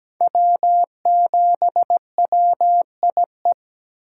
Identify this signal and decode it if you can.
Morse code, heard as a keyed tone, about 17 wpm, 700 Hz. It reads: W7WIE
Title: USA Call Signs - 5 Characters - 4x Character Spacing - 17wpm